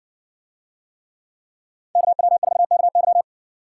Morse code, heard as a keyed tone, about 55 wpm, 700 Hz. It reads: KW4KY